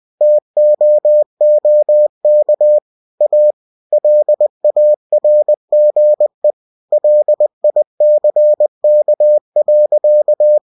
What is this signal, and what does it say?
Morse code, heard as a keyed tone, about 20 wpm, 600 Hz. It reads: TOOK A LARGE LICK.
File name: cw-books / winnie-the-pooh / winnie-20wpm-page-0037